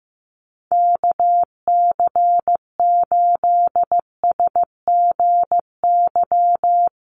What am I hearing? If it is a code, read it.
Morse code, heard as a keyed tone, about 14 wpm, 700 Hz. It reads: KC8SGY
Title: USA Call Signs Repeated 2x 15wpm